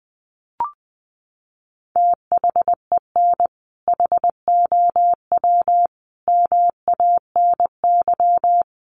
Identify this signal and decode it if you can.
Morse code, heard as a keyed tone, about 20 wpm, 700 Hz. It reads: THEN HOW MANY
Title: Sentences from Top 100 Words 20wpm